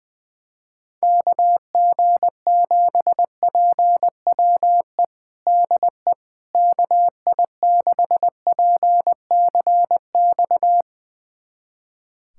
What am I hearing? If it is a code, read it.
Morse code, heard as a keyed tone, about 20 wpm, 700 Hz. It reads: KG7PWE DE KI6PCX
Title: USA Callsign DE Callsign - 6 Characters - 20wpm